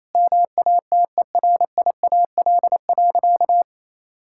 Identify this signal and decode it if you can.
Morse code, heard as a keyed tone, about 28 wpm, 700 Hz. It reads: MATERIAL.
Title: Sentences from Top 1000 Words - Review Part 5 - 28wpm